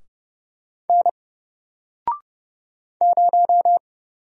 Morse code, heard as a keyed tone, about 30 wpm, 700 Hz. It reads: N 0
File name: Single-Letter-Number-30wpm